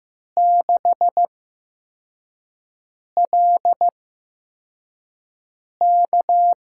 Morse code, heard as a keyed tone, about 15 wpm, spaced as for 4 wpm, 700 Hz. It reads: 6LK
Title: USA Call Signs - 5 Characters - 8x Character Spacing - 15wpm